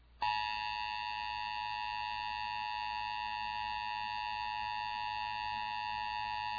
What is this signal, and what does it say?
Emergency Alert System (SAME) tones (the 853 and 960 Hz attention tone), with noise. It is heard from 0.2 to 6.0 s.